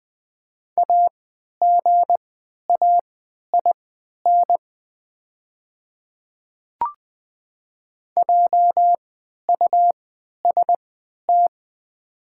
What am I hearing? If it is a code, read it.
Morse code, heard as a keyed tone, about 20 wpm, spaced as for 11 wpm, 690 Hz. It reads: AGAIN JUST
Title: 3x Character Spacing - Top 200 Words - 20wpm